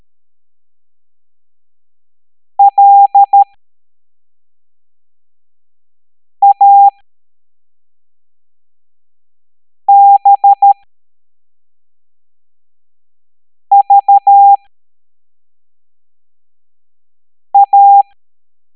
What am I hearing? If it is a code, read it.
Morse code, heard as a keyed tone, about 13 wpm, spaced as for 3 wpm, 800 Hz. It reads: LABVA
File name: hamdata / L09-EX04